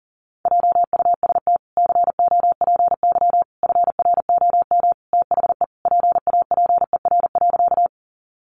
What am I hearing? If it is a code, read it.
Morse code, heard as a keyed tone, about 40 wpm, 700 Hz. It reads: JUST COPY FROM THE PAPER.
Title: Sentences from Top 1400 Words 40wpm